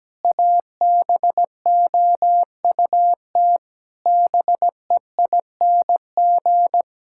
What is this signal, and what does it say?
Morse code, heard as a keyed tone, about 17 wpm, 700 Hz. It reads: ABOUT BEING
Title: Sentences from Top 1000 Words - Review Part 6 - 17wpm